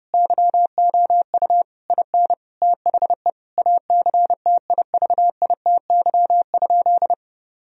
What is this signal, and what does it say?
Morse code, heard as a keyed tone, about 32 wpm, 700 Hz. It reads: YOU IN THE ACTIVITY?
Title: Sentences from Top 1000 Words - Review Part 4 - 30wpm